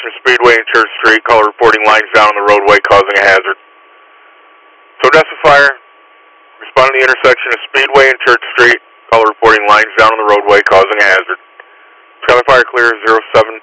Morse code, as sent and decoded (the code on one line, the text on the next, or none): none